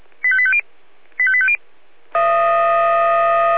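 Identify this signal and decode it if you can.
Five-tone selective call (ZVEI), heard as quick runs of five tones, tones of about 70 ms. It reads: 85850 85850
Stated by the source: ZVEI five-tone selective call 85865 received by SDR.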